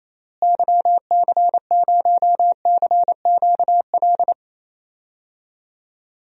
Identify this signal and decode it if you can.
Morse code, heard as a keyed tone, about 28 wpm, 700 Hz. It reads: YC0CQL
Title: DXCC-100 Call Signs - 6 Characters - Repeated 3x - 28wpm